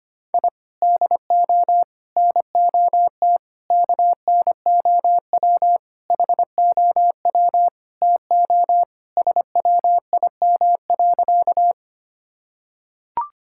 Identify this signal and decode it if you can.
Morse code, heard as a keyed tone, about 25 wpm, 700 Hz. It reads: I DO NOT KNOW HOW TO SWIM.
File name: practice / 25wpm / Sentences with 4-Word Phrases - 25wpm